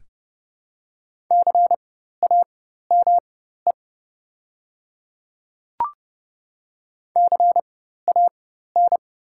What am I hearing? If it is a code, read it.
Morse code, heard as a keyed tone, about 30 wpm, spaced as for 14 wpm, 700 Hz. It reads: CAME CAN
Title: 4x Character Spacing - Top 500 Words - 30wpm